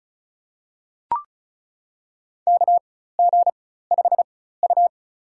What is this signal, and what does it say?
Morse code, heard as a keyed tone, about 34 wpm, spaced as for 16 wpm, 700 Hz. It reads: KG5U
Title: USA Call Signs - 4 Characters - 4x Character Spacing - 35wpm